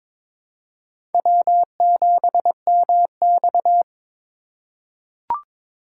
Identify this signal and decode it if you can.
Morse code, heard as a keyed tone, about 22 wpm, 700 Hz. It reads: W7MX
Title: USA Call Signs - 4 Characters - 2x Character Spacing - 22wpm